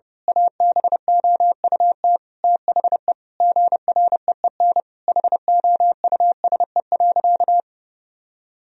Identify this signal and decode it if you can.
Morse code, heard as a keyed tone, about 32 wpm, 700 Hz. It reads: ABOUT THE GREEN HOUSE.
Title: Sentences from Top 1000 Words - Review Part 2 - 30wpm